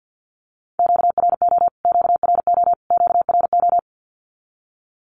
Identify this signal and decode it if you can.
Morse code, heard as a keyed tone, about 48 wpm, 700 Hz. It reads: QRO QRO QRO
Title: Q-Codes Repeated 3x 50wpm